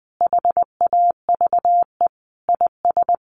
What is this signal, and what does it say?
Morse code, heard as a keyed tone, about 19 wpm, 700 Hz. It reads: HAVE IS
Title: Sentences from Top 1500 Words 20wpm